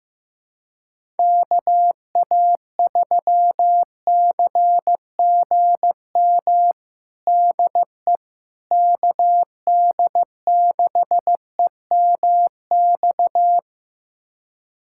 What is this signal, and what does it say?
Morse code, heard as a keyed tone, about 15 wpm, 700 Hz. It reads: KA3CGM DE KD6EMX
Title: USA Callsign DE Callsign 15wpm